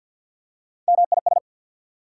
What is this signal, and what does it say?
Morse code, heard as a keyed tone, about 54 wpm, 690 Hz. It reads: MIS
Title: Prefixes and Suffixes 50wpm